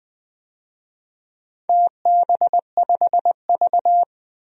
Morse code, heard as a keyed tone, about 20 wpm, 700 Hz. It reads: TB5V
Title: DXCC-100 Call Signs - 4 Characters - 20wpm